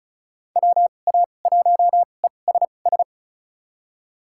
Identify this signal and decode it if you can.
Morse code, heard as a keyed tone, about 34 wpm, spaced as for 25 wpm, 700 Hz. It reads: WA1ESS
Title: USA Call Signs - 2x Character Spacing - 35wpm